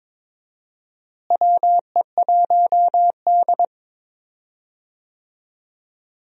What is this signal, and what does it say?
Morse code, heard as a keyed tone, about 22 wpm, 700 Hz. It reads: WE1D
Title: USA Call Signs - 4 Characters - Repeated 3x - 22wpm